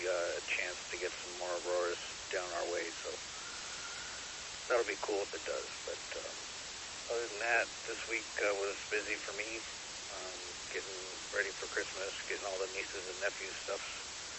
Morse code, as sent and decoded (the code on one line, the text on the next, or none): none